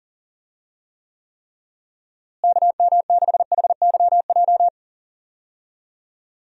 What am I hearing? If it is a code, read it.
Morse code, heard as a keyed tone, about 37 wpm, 700 Hz. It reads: KM6HYJ
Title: USA Call Signs - 6 Characters - Repeated 3x - 40wpm